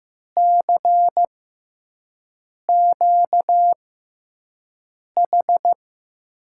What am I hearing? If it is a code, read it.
Morse code, heard as a keyed tone, about 15 wpm, spaced as for 5 wpm, 700 Hz. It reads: CQH
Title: USA Call Signs - 6x Character Spacing - 15wpm